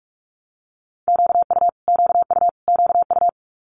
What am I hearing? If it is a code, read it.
Morse code, heard as a keyed tone, about 46 wpm, 700 Hz. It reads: QU QU QU
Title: Multi-letter Phonemes Repeated 3x 45wpm